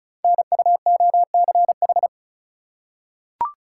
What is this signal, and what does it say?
Morse code, heard as a keyed tone, about 36 wpm, 700 Hz. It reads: NUOCH
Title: Code Groups - 5 Letters - 35wpm